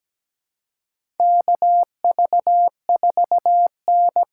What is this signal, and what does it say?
Morse code, heard as a keyed tone, about 17 wpm, 700 Hz. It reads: KV4N
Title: USA Call Signs - 6x Character Spacing - 17wpm